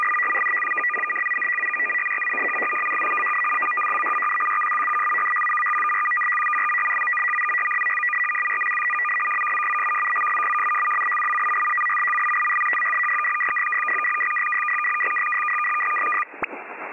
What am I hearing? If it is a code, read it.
Two-tone teleprinter signal, about 450 Hz shift.
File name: POL- -1410utc-fsk100-740